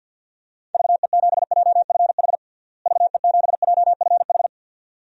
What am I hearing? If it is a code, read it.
Morse code, heard as a keyed tone, about 53 wpm, 700 Hz. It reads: VE7JFH VE7JFH
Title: DXCC-100 Call Signs - 6 Characters - Repeated 2x - 50wpm